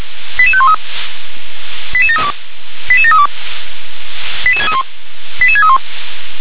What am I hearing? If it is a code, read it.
Five-tone selective call (ZVEI), heard as quick runs of five tones, tones of about 70 ms. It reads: 88513 88732 88732 88631